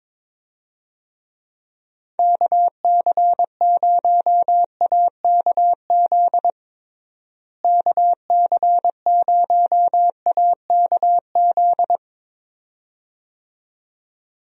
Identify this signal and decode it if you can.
Morse code, heard as a keyed tone, about 22 wpm, 700 Hz. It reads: KC0AKZ KC0AKZ
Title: USA Call Signs - 6 Characters - Repeated 2x - 22wpm